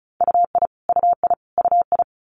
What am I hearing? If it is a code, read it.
Morse code, heard as a keyed tone, about 36 wpm, 700 Hz. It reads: UI UI UI
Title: Multi-letter Phonemes Repeated 3x 35wpm